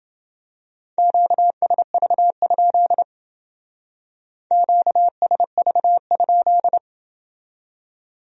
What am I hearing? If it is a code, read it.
Morse code, heard as a keyed tone, about 28 wpm, 700 Hz. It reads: QSV? QSV?